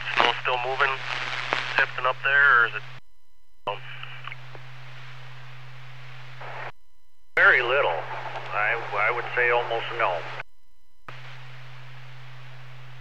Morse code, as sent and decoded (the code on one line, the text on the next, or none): none